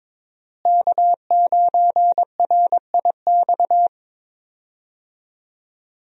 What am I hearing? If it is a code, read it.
Morse code, heard as a keyed tone, about 22 wpm, 700 Hz. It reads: K9RIX
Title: USA Call Signs - 5 Characters - Repeated 2x - 22wpm